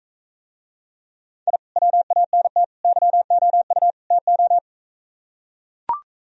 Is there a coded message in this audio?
Yes: Morse code from 1.5 to 4.6 s, heard as a keyed tone, about 45 wpm, 700 Hz. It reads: I WANT YOU TO